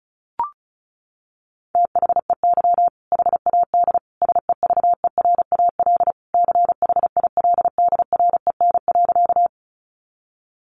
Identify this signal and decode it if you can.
Morse code, heard as a keyed tone, about 34 wpm, 700 Hz. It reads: THEY HAD SEVERAL CHILDREN.
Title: Sentences from Top 400 Words 35wpm